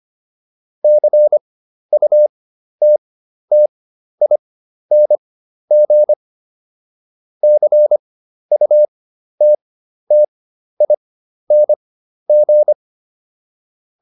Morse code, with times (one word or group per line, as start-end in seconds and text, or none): 0.8-6.1 CUTTING
7.4-12.7 CUTTING